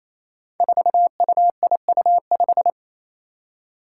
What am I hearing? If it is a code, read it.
Morse code, heard as a keyed tone, about 28 wpm, 700 Hz. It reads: 4UIU5
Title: Five Challenging Letter-Number Combinations 28wpm